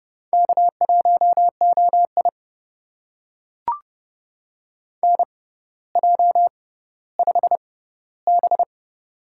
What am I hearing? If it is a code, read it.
Morse code, heard as a keyed tone, about 31 wpm, spaced as for 16 wpm, 700 Hz. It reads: K1OI NJ5B
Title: USA Call Signs - 4 Characters - 6x Character Spacing - 30wpm